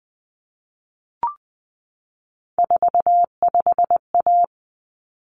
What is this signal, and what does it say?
Morse code, heard as a keyed tone, about 20 wpm, 700 Hz. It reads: 45A
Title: Common Cut Numbers - 3-Digit 20wpm